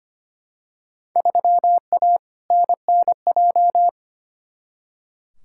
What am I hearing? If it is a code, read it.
Morse code, heard as a keyed tone, about 25 wpm, 700 Hz. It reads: 3A NNJ
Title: Field Day Exchanges - 25wpm